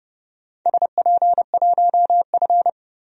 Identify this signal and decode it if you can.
Morse code, heard as a keyed tone, about 30 wpm, 700 Hz. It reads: SP1F
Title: DXCC-100 Call Signs - 4 Characters - 30wpm